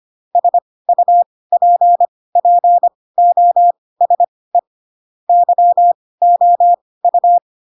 Morse code, heard as a keyed tone, about 25 wpm, spaced as for 18 wpm, 700 Hz. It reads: SUPPOSE YOU
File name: Pinocchio / Pinocchio ch2 12 25 ik1hge